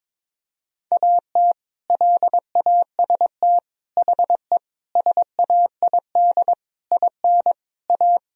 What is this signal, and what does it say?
Morse code, heard as a keyed tone, about 22 wpm, 700 Hz. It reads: AT LAST HE SAID IN A
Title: Sentences from Top 1000 Words - Review Part 1 - 22wpm